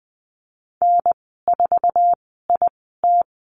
Morse code, heard as a keyed tone, about 20 wpm, 700 Hz. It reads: N 4 I T
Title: 154 Lesson 25g - Letter p - Call Signs 20wpm